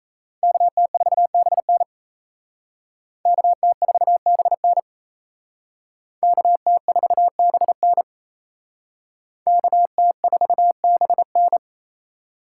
Morse code, heard as a keyed tone, about 36 wpm, 700 Hz. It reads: KT4BN KT4BN KT4BN KT4BN